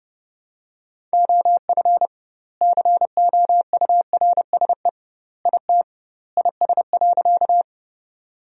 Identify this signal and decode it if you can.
Morse code, heard as a keyed tone, about 30 wpm, 700 Hz. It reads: OF COURSE IT IS.
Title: 2x Word-Spacing - Sentences from Top 500 Words - 30wpm